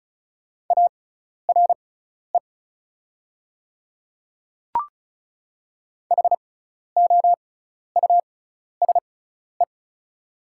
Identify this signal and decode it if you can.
Morse code, heard as a keyed tone, about 36 wpm, spaced as for 12 wpm, 690 Hz. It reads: ARE HOUSE
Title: 6x Character Spacing - Top 300 Words - 35wpm